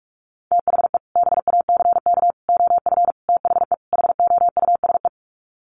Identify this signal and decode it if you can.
Morse code, heard as a keyed tone, about 44 wpm, 700 Hz. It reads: THE BACK OF THE HOUSE
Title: Sets of 5 Words Encore2 45wpm